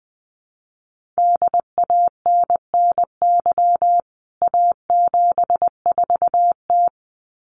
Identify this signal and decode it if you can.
Morse code, heard as a keyed tone, about 20 wpm, 700 Hz. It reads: DANNY A74T
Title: CWT Exchange 20wpm